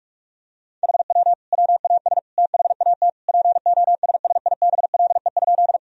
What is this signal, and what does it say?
Morse code, heard as a keyed tone, about 41 wpm, 700 Hz. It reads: HW WAS THAT POSSIBLE?
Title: Sentences from Top 500 Words and Abbreviations 45wpm